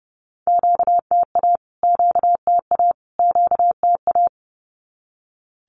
Morse code, heard as a keyed tone, about 30 wpm, 700 Hz. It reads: QTA QTA QTA
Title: Q-Codes Repeated 3x 30wpm